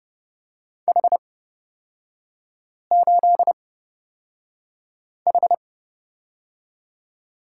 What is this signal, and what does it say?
Morse code, heard as a keyed tone, about 29 wpm, spaced as for 5 wpm, 700 Hz. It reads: H8H
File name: Lesson 37e - Letter x - Three Characters 30wpm